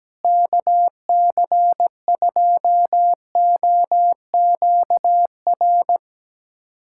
Morse code, heard as a keyed tone, about 17 wpm, 700 Hz. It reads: KC2OQR
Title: USA Callsign DE Callsign 17wpm